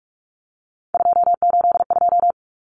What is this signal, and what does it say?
Morse code, heard as a keyed tone, about 44 wpm, 700 Hz. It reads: 282
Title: Numbers - 3-Digit 45wpm